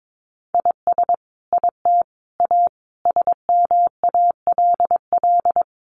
Morse code, heard as a keyed tone, about 22 wpm, 700 Hz. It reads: IS IT A SMALL
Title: Sentences from Top 1500 Words 22wpm